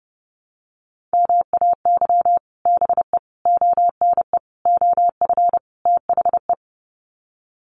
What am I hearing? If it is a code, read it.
Morse code, heard as a keyed tone, about 30 wpm, 700 Hz. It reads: MAY BE ONE OF THE
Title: Sets of 5 Words Encore2 30wpm